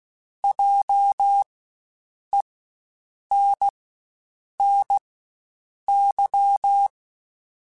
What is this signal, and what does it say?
Morse code, heard as a keyed tone, about 16 wpm, spaced as for 7 wpm, 800 Hz. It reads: JENNY
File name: Spaced Alphas / Spaced Alphas mp3s / Js sp @15